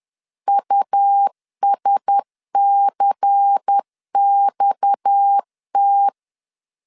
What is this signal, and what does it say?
Morse code, heard as a keyed tone, about 10 wpm, 800 Hz. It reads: USCXT